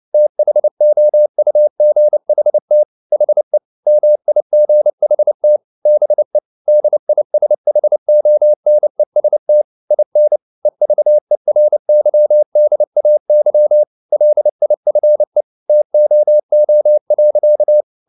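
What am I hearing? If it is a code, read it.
Morse code, heard as a keyed tone, about 29 wpm, 600 Hz. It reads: THOUGHT HE MIGHT BE DISHONEST IN EVERYDAY LIFE TOO.
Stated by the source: Morse code at 29 wpm, sending Ken Follett's novel Never in English, from CW med Gnister.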